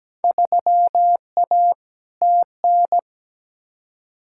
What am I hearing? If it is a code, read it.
Morse code, heard as a keyed tone, about 17 wpm, 700 Hz. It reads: 3A TN
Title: Field Day Exchanges - 17wpm